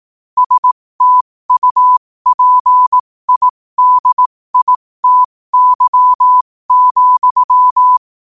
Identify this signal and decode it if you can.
Morse code, heard as a keyed tone, about 18 wpm, spaced as for 15 wpm, 1000 Hz. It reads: STUPIDITY,
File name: qotdfiles / 15wpm